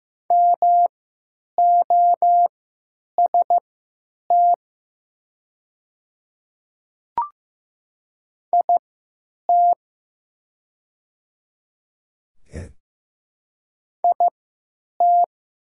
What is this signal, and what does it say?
Morse code, heard as a keyed tone, about 15 wpm, spaced as for 9 wpm, 700 Hz. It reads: MOST IT IT
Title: 3x Character Spacing - Top 100 Words - 15wpm